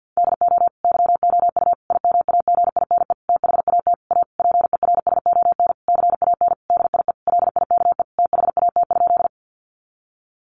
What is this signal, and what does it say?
Morse code, heard as a keyed tone, about 50 wpm, 700 Hz. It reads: DO YOU IMAGINE THAT A PERSON CAN DIE LIKE THAT?
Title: Sentences from Top 1400 Words 50wpm